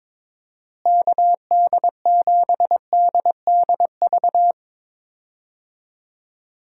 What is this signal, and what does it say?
Morse code, heard as a keyed tone, about 22 wpm, 700 Hz. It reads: KD7DDV
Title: USA Call Signs - 6 Characters - Repeated 2x - 22wpm